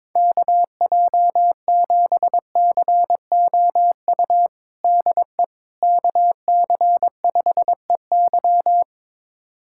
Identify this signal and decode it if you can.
Morse code, heard as a keyed tone, about 22 wpm, 700 Hz. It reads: KJ7COU DE KC5EY
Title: USA Callsign DE Callsign 22wpm